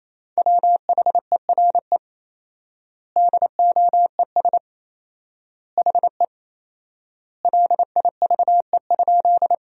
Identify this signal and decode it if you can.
Morse code, heard as a keyed tone, about 27 wpm, 700 Hz. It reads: WHERE DOES HE LIVE?
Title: 4x Word-Spacing - Sentences from Top 200 Words - 28wpm